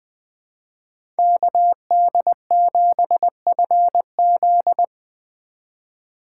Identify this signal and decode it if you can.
Morse code, heard as a keyed tone, about 20 wpm, 700 Hz. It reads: KD7FZ
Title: USA Call Signs - 5 Characters - 3x Character Spacing - 20wpm